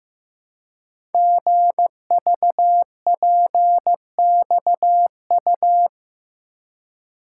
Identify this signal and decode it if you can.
Morse code, heard as a keyed tone, about 15 wpm, 700 Hz. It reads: GVPXU
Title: Code Groups - 5 Letters - 15wpm